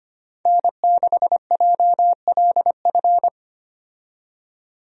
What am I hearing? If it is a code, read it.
Morse code, heard as a keyed tone, about 25 wpm, 700 Hz. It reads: N6JLF
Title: USA Call Signs - 5 Characters - 25wpm